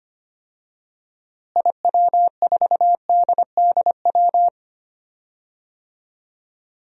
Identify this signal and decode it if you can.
Morse code, heard as a keyed tone, about 25 wpm, 700 Hz. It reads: IW4DDW
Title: DXCC-100 Call Signs - 6 Characters - Repeated 2x - 25wpm